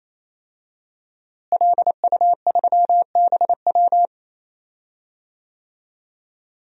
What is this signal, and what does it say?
Morse code, heard as a keyed tone, about 28 wpm, 700 Hz. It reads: LU3BW
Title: DXCC-100 Call Signs - 5 Characters - Repeated 3x - 28wpm